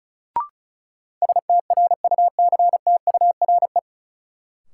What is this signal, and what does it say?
Morse code, heard as a keyed tone, about 36 wpm, 700 Hz. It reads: STRUCTURE
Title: Sentences from Top 1000 Words - Review Part 6 - New Word Drill - 35wpm